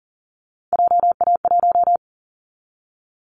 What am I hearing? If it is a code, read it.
Morse code, heard as a keyed tone, about 40 wpm, 700 Hz. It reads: JA1
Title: Callsign Prefixes for Top 50 DXCC - 40wpm